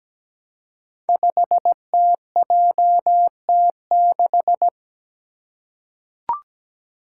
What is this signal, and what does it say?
Morse code, heard as a keyed tone, about 17 wpm, 700 Hz. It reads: 5TJT6
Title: Five Challenging Letter-Number Combinations 17wpm